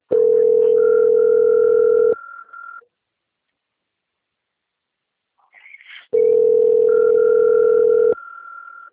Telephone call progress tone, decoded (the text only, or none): ringback tone